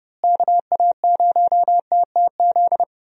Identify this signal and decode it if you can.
Morse code, heard as a keyed tone, about 30 wpm, 700 Hz. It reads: KA0TTZ
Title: USA Call Signs - 6 Characters - Repeated 2x - 30wpm